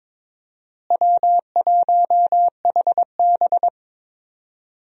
Morse code, heard as a keyed tone, about 22 wpm, 700 Hz. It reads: W1HB
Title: USA Call Signs - 4 Characters - Repeated 3x - 22wpm